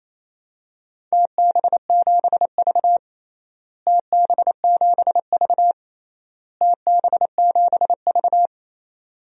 Morse code, heard as a keyed tone, about 28 wpm, 700 Hz. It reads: TB7V TB7V TB7V
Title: DXCC-100 Call Signs - 4 Characters - Repeated 3x - 28wpm